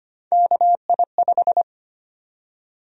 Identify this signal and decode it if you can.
Morse code, heard as a keyed tone, about 25 wpm, 700 Hz. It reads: KI5
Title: USA Call Sign Prefixes - Repeated 2x - 25wpm